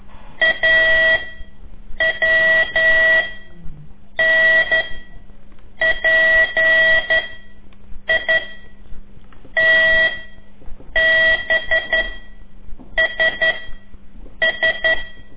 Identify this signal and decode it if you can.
Morse code, heard as a keyed tone, about 11 wpm, 1950 Hz. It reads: A W N P I T B S S
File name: CORSI CW